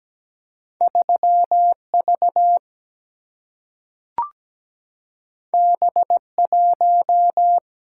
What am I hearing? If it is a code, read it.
Morse code, heard as a keyed tone, about 17 wpm, 700 Hz. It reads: 3V B1
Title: Single-Letter Single-Number 17wpm